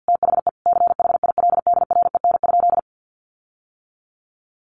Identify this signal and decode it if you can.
Morse code, heard as a keyed tone, about 50 wpm, 700 Hz. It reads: THE CHILDREN?
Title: Sentences from Top 1400 Words 50wpm